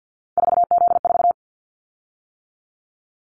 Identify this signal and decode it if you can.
Morse code, heard as a keyed tone, about 50 wpm, 700 Hz. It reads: 4Z4
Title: Callsign Prefixes for Top 50 DXCC - 50wpm